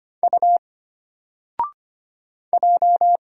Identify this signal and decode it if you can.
Morse code, heard as a keyed tone, about 25 wpm, 700 Hz. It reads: U J
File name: Single-Letter-Number-25wpm